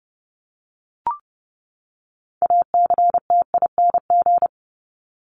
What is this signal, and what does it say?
Morse code, heard as a keyed tone, about 30 wpm, 700 Hz. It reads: ACTING
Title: ING Suffix - Top 100 Words with Base Words 30wpm